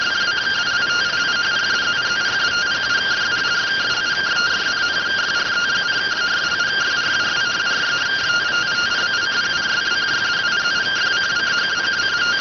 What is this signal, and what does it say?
Two-tone teleprinter signal, about 170 Hz shift.